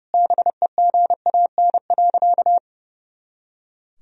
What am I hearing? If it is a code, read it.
Morse code, heard as a keyed tone, about 30 wpm, 700 Hz. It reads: BEGAN.
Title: Sentences from Top 400 Words and Abbreviations 30wpm